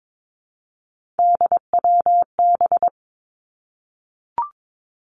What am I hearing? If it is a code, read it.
Morse code, heard as a keyed tone, about 22 wpm, 700 Hz. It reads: DWB
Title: Three Challenging Letter-Number Combinations 22wpm